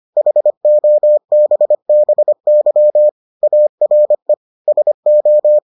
Morse code, heard as a keyed tone, about 25 wpm, 600 Hz. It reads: HOBBY ARE SO